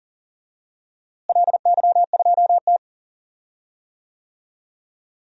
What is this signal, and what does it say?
Morse code, heard as a keyed tone, about 40 wpm, 700 Hz. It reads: LY2T
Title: DXCC-100 Call Signs - 4 Characters - Repeated 2x - 40wpm